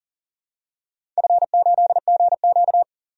Morse code, heard as a keyed tone, about 38 wpm, 700 Hz. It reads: F8GQ
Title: DXCC-100 Call Signs - 4 Characters - 40wpm